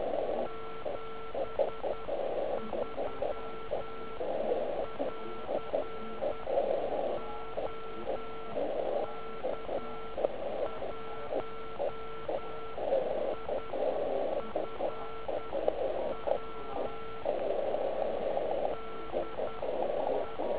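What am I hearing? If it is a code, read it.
Morse code, heard as a keyed tone, about 10 wpm, spaced as for 8 wpm, 1340 Hz. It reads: 73 <AR> OK1IFW DE
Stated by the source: Slow QRS Morse code beginner practice skeds on 3559.5 kHz.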